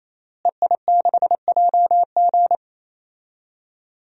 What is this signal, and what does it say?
Morse code, heard as a keyed tone, about 27 wpm, 700 Hz. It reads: EI6JG
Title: DXCC-100 Call Signs - 5 Characters - Repeated 3x - 28wpm